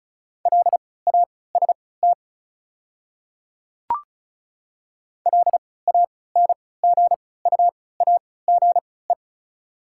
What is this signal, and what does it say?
Morse code, heard as a keyed tone, about 36 wpm, spaced as for 20 wpm, 690 Hz. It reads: LAST LANGUAGE
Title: 3x Character Spacing - Top 500 Words - 35wpm